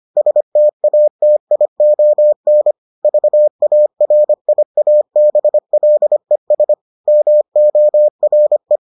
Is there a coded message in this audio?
Yes: Morse code from 0.2 to 8.8 s, heard as a keyed tone, about 25 wpm, 600 Hz. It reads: STATION VARIABLES MORE